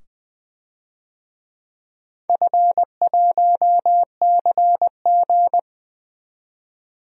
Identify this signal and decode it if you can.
Morse code, heard as a keyed tone, about 20 wpm, 700 Hz. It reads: F1CG
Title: DXCC-100 Call Signs - 4 Characters - 20wpm